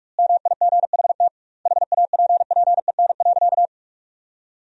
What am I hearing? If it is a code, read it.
Morse code, heard as a keyed tone, about 46 wpm, 700 Hz. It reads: MIGHT HAPPEN.
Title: 2x Word-Spacing - Sentences from Top 400 Words - 45wpm